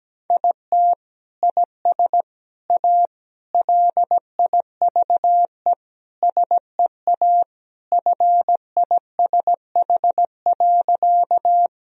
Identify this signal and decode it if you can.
Morse code, heard as a keyed tone, about 17 wpm, 700 Hz. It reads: IT IS A LIVE SEA FISH.
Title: Sentences from Top 1000 Words - Review Part 1 - 17wpm